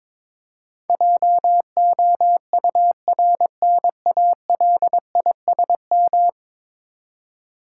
Morse code, heard as a keyed tone, about 22 wpm, 700 Hz. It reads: JOURNALISM
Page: morse-code-ninja-practice-22wpm